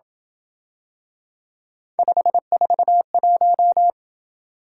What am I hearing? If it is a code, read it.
Morse code, heard as a keyed tone, about 27 wpm, 700 Hz. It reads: 541